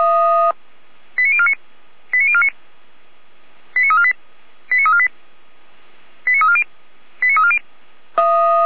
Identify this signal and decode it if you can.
Five-tone selective call (ZVEI), heard as quick runs of five tones, tones of about 70 ms. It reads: 89049 89049 89348 89348 89340 89340